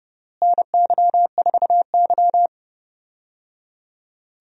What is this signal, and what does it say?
Morse code, heard as a keyed tone, about 30 wpm, 700 Hz. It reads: NY4Y
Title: USA Call Signs - 4 Characters - Repeated 3x - 30wpm